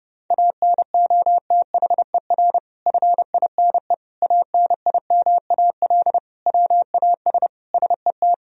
Morse code, heard as a keyed tone, about 32 wpm, 700 Hz. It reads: ANOTHER FINE ANIMAL WAS SET
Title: Sentences from Top 1000 Words - Review Part 1 - 30wpm